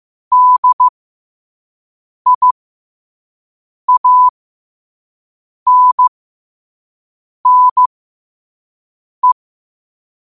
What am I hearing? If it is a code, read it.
Morse code, heard as a keyed tone, about 15 wpm, spaced as for 5 wpm, 1000 Hz. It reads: DIANNE
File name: qotdfiles / 5wpm